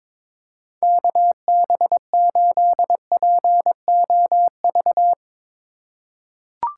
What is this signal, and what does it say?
Morse code, heard as a keyed tone, about 22 wpm, 700 Hz. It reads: KB8POV
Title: USA Call Signs - 6 Characters - 4x Character Spacing - 22wpm